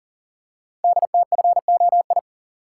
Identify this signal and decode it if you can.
Morse code, heard as a keyed tone, about 42 wpm, 690 Hz. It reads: DTFOI
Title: Code Groups - 5 Letters - 40wpm